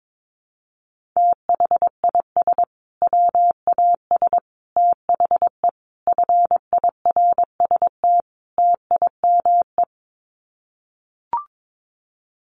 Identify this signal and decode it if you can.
Morse code, heard as a keyed tone, about 22 wpm, 700 Hz. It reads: THIS WAS THE FIRST TIME
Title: Sets of 5 Words Encore 22wpm